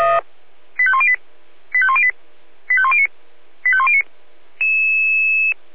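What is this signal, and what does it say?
Five-tone selective call (ZVEI), heard as quick runs of five tones, tones of about 70 ms. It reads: 85108 85108 85109 85109
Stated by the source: ZVEI five-tone selective call received by SDR.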